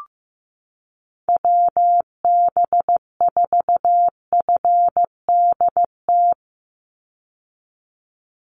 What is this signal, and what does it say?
Morse code, heard as a keyed tone, about 15 wpm, 700 Hz. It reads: WB4FDT
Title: Call Signs Repeated 3x - 15wpm